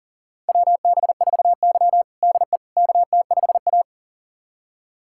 Morse code, heard as a keyed tone, about 37 wpm, 700 Hz. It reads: WB4Y DE KT5A
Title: USA Callsign DE Callsign - 4 Characters - 40wpm